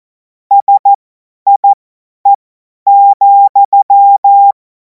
Morse code, heard as a keyed tone, about 14 wpm, spaced as for 10 wpm, 800 Hz. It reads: SIE,
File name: Morsekurs / Witze / Witz 14. Juli 2026 50 F70